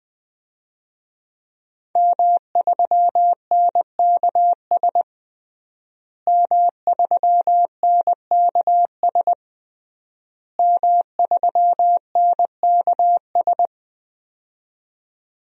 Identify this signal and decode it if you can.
Morse code, heard as a keyed tone, about 20 wpm, 700 Hz. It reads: M3NKS M3NKS M3NKS
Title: DXCC-100 Call Signs - 5 Characters - Repeated 3x - 20wpm